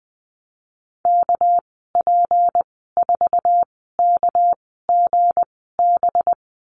Morse code, heard as a keyed tone, about 20 wpm, spaced as for 14 wpm, 700 Hz. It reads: KP4KGB
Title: 231 Lesson 37g - Letter x - Call Signs 20wpm